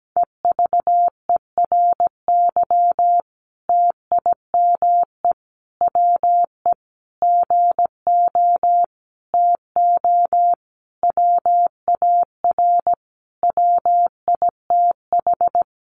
Morse code, heard as a keyed tone, about 17 wpm, 700 Hz. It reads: EVERY TIME WE GO TO WAR WITH
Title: Sentences from Top 1300 Words 17wpm